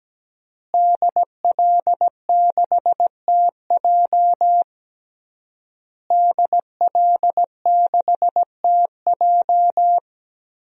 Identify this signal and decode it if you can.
Morse code, heard as a keyed tone, about 17 wpm, 700 Hz. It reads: DL6TJ DL6TJ
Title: DXCC-100 Call Signs - 5 Characters - Repeated 3x - 17wpm